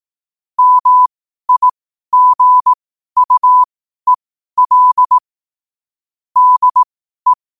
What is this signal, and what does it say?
Morse code, heard as a keyed tone, about 18 wpm, spaced as for 12 wpm, 1000 Hz. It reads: MIGUEL DE
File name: qotdfiles / 12wpm